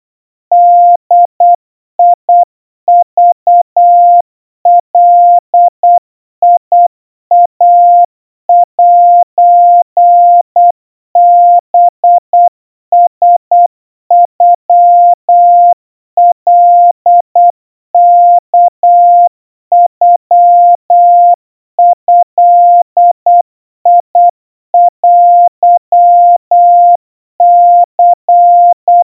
Morse code (Japanese Wabun code), heard as a keyed tone, about 8 wpm, 700 Hz. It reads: ボクガイセハラノカワノドテニ
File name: 30 cpm 700 Hz Crispy